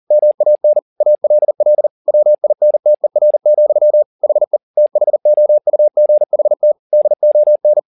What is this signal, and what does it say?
Morse code, heard as a keyed tone, about 40 wpm, 600 Hz. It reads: MAN ALL WINTER, HE THOUGHT DON
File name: the old man and the sea 2 40wpm